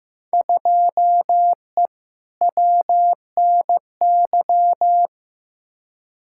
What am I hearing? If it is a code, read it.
Morse code, heard as a keyed tone, about 15 wpm, 700 Hz. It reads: 2E WNY
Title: Field Day Exchanges Encore - 15wpm